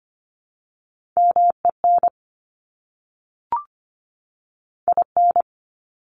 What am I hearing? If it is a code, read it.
Morse code, heard as a keyed tone, about 25 wpm, 700 Hz. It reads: MEN IN
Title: Top 500 Words 25wpm